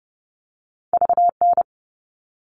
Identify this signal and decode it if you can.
Morse code, heard as a keyed tone, about 30 wpm, 700 Hz. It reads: VN
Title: All Cut Numbers - 2-Digit 30wpm